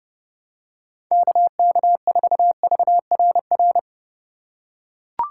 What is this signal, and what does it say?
Morse code, heard as a keyed tone, about 30 wpm, 700 Hz. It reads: KK4VRR
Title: USA Call Signs - 6 Characters - 4x Character Spacing - 30wpm